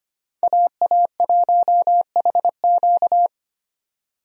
Morse code, heard as a keyed tone, about 25 wpm, 700 Hz. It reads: AA1HQ
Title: USA Call Signs - 5 Characters - 25wpm